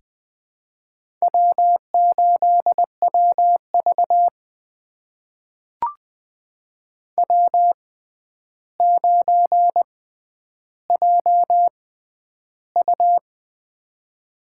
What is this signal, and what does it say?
Morse code, heard as a keyed tone, about 20 wpm, spaced as for 10 wpm, 700 Hz. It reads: W8WV W9JU
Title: USA Call Signs - 4 Characters - 6x Character Spacing - 20wpm